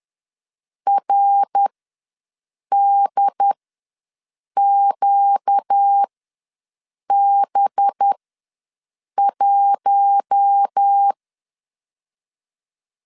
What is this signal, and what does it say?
Morse code, heard as a keyed tone, about 10 wpm, spaced as for 6 wpm, 800 Hz. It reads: RDQB1
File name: LEZIONI / 5 LBVC / 5 LB ws 10wpm 3616